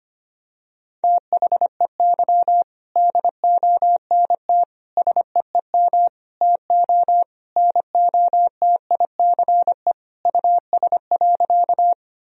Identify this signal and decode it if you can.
Morse code, heard as a keyed tone, about 25 wpm, 700 Hz. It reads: THEY DONT SEEM TO NOTICE US.
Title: Sentences from Top 1000 Words - Review Part 2 - 25wpm